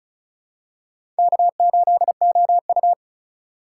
Morse code, heard as a keyed tone, about 34 wpm, 700 Hz. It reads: K8OU
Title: USA Call Signs - 4 Characters - 3x Character Spacing - 35wpm